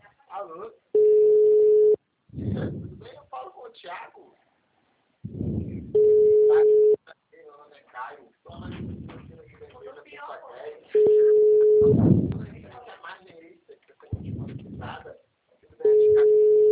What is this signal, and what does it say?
Ringback tone (Europe, 425 Hz), with speech.
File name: out- -2024- - - .5225